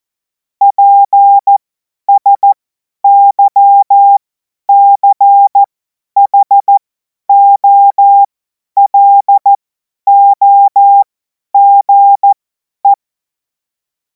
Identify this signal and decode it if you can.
Morse code, heard as a keyed tone, about 14 wpm, spaced as for 10 wpm, 800 Hz. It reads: PSYCHOLOGE
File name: 50 F70